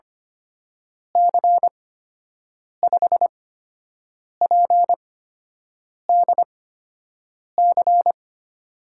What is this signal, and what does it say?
Morse code, heard as a keyed tone, about 25 wpm, spaced as for 7 wpm, 700 Hz. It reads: C5PDC
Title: USA Call Signs - 6 Characters - 8x Character Spacing - 25wpm